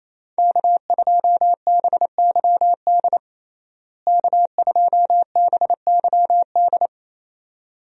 Morse code, heard as a keyed tone, about 28 wpm, 700 Hz. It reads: K2BYD K2BYD
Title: USA Call Signs - 5 Characters - Repeated 2x - 28wpm